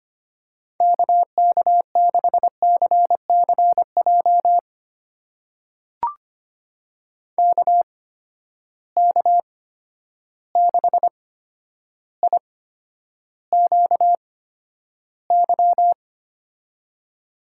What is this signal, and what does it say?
Morse code, heard as a keyed tone, about 25 wpm, spaced as for 11 wpm, 700 Hz. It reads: KK6CCJ KK6IQY
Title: USA Call Signs - 6 Characters - 8x Character Spacing - 25wpm